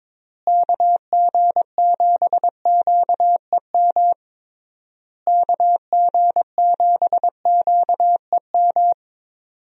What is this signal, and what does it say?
Morse code, heard as a keyed tone, about 22 wpm, 700 Hz. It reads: KG7QEM KG7QEM
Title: USA Call Signs - 6 Characters - Repeated 3x - 22wpm